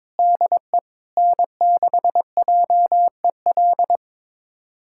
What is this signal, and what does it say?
Morse code, heard as a keyed tone, about 22 wpm, 700 Hz. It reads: DE N6JEL
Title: USA Callsign DE Callsign - 5 Characters - 22wpm